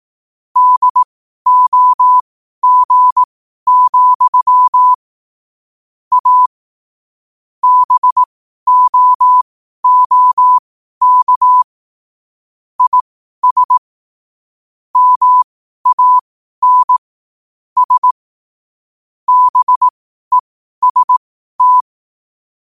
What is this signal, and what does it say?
Morse code, heard as a keyed tone, about 18 wpm, spaced as for 12 wpm, 1000 Hz. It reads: DOG, A BOOK IS MAN S BEST
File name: qotdfiles / 12wpm